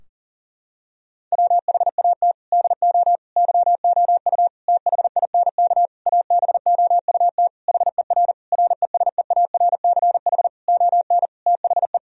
Morse code, heard as a keyed tone, about 40 wpm, 700 Hz. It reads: WHAT DO YOU THINK ABOUT HER RESEARCH ON THE
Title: Sentences from Top 1000 Words - Review Part 4 - 40wpm